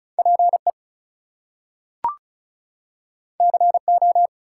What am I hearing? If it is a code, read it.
Morse code, heard as a keyed tone, about 36 wpm, 700 Hz. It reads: PE CO
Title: US States and Canadian Provinces 35wpm